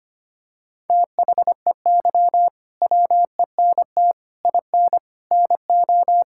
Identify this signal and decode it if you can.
Morse code, heard as a keyed tone, about 25 wpm, 700 Hz. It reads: THEY WENT IN NO
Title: Sentences from Top 1000 Words - Review Part 5 - 25wpm